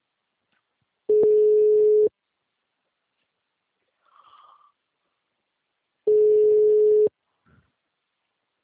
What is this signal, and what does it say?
Ringback tone (Europe, 425 Hz).
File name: out- -2016